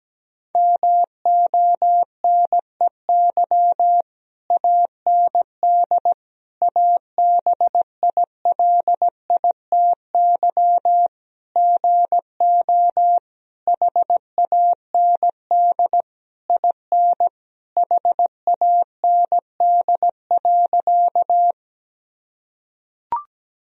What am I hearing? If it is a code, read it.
Morse code, heard as a keyed tone, about 17 wpm, 700 Hz. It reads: MONEY AND ABILITY GO HAND IN HAND.
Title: Sentences from Top 1000 Words - Review Part 3 - 17wpm